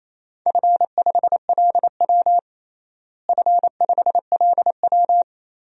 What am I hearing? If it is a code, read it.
Morse code, heard as a keyed tone, about 28 wpm, 700 Hz. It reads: F5LW F5LW
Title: DXCC-100 Call Signs - 4 Characters - Repeated 3x - 28wpm